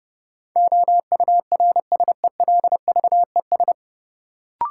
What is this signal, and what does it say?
Morse code, heard as a keyed tone, about 30 wpm, 700 Hz. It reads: OURSELVES
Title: Sentences from Top 1600 Words - New Word Drill - 30wpm